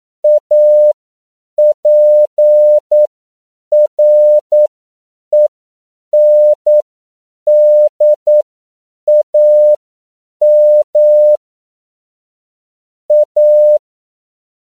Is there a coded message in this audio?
Yes: Morse code from 0.2 to 13.8 s, heard as a keyed tone, about 9 wpm, spaced as for 7 wpm, 600 Hz. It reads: APRENDAM A